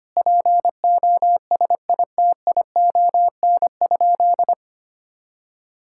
Morse code, heard as a keyed tone, about 25 wpm, 700 Hz. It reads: POSITION?
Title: Sentences from Top 1000 Words - Review Part 5 - 25wpm